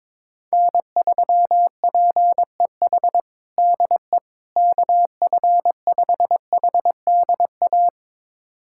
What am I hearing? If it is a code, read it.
Morse code, heard as a keyed tone, about 22 wpm, 700 Hz. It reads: N3PEH DE KF5HDA
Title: USA Callsign DE Callsign 22wpm